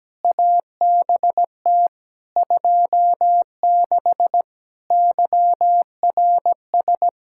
Morse code, heard as a keyed tone, about 17 wpm, 700 Hz. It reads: ABT 26 YRS
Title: K1USN SST QSO Practice 17wpm